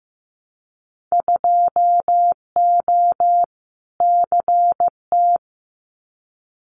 Morse code, heard as a keyed tone, about 15 wpm, 700 Hz. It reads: 2O CT
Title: Winter Field Day Exchanges 15wpm